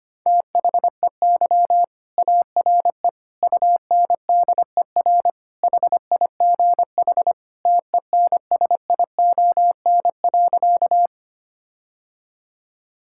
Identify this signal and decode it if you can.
Morse code, heard as a keyed tone, about 25 wpm, 700 Hz. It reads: THEY ARE UNDER HIGH TENSION.
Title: Sentences from Top 1600 Words 25wpm